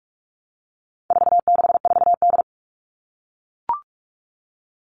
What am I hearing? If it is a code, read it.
Morse code, heard as a keyed tone, about 44 wpm, 700 Hz. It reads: 464D
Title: All Cut Numbers - 4-Digit 45wpm